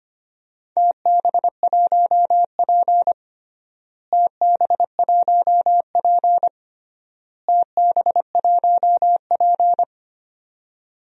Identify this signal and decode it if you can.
Morse code, heard as a keyed tone, about 25 wpm, 700 Hz. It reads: TB1P TB1P TB1P
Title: DXCC-100 Call Signs - 4 Characters - Repeated 3x - 25wpm